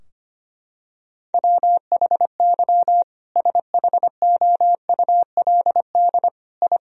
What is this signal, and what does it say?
Morse code, heard as a keyed tone, about 25 wpm, 700 Hz. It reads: WHY SHOULD I
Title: Sentences from Top 1000 Words - Review Part 3 - 25wpm